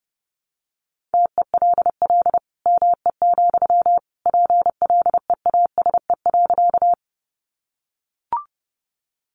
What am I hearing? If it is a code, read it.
Morse code, heard as a keyed tone, about 30 wpm, 700 Hz. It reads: TELL ME, PLEASE.
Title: Sentences from Top 1500 Words 30wpm